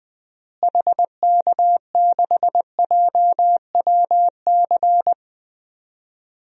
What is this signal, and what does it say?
Morse code, heard as a keyed tone, about 20 wpm, 700 Hz. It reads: HK6JWC
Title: DXCC-100 Call Signs - 6 Characters - Repeated 3x - 20wpm